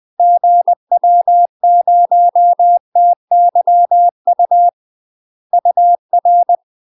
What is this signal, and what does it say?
Morse code, heard as a keyed tone, about 20 wpm, 700 Hz. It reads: GW0TYU UR
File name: UK MASTER QSOs TRAINING / 20WPM / UK MASTER QSOs 0065